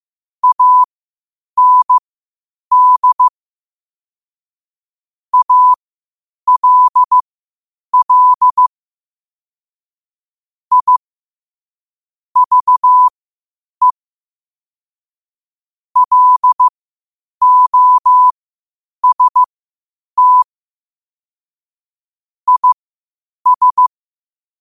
Morse code, heard as a keyed tone, about 15 wpm, spaced as for 8 wpm, 1000 Hz. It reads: AND ALL I VE LOST IS
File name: qotdfiles / 8wpm